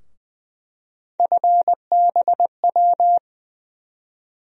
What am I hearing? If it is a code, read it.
Morse code, heard as a keyed tone, about 20 wpm, 700 Hz. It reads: FBW
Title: USA Call Sign Postfixes - 3 Letters 20wpm